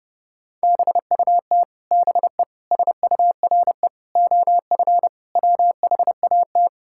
Morse code, heard as a keyed tone, about 30 wpm, 700 Hz. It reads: BUT BE SURE OF WHAT
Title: Sentences from Top 1600 Words 30wpm